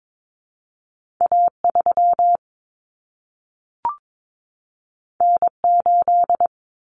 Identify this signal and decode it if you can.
Morse code, heard as a keyed tone, about 22 wpm, 700 Hz. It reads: A3 N8
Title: Common Cut Numbers - 2-Digit 22wpm